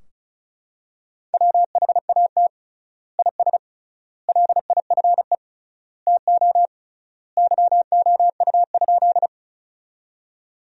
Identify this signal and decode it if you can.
Morse code, heard as a keyed tone, about 36 wpm, 700 Hz. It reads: WHAT IS LIFE TO YOU?
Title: 3x Word-Spacing - Sentences from Top 300 Words - 35wpm